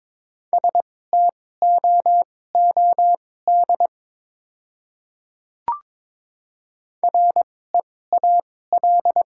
Morse code, heard as a keyed tone, about 22 wpm, spaced as for 16 wpm, 700 Hz. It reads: STOOD REAL
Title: 2x Character Spacing - Top 500 Words - 22wpm